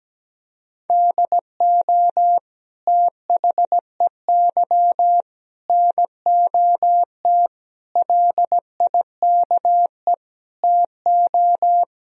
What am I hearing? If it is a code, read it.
Morse code, heard as a keyed tone, about 17 wpm, 700 Hz. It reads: DO THEY NOT LIKE TO
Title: Sentences from Top 1000 Words - Review Part 4 - 17wpm